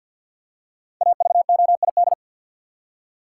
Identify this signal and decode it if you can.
Morse code, heard as a keyed tone, about 53 wpm, 700 Hz. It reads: AVOID
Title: Five Letter Words - 50wpm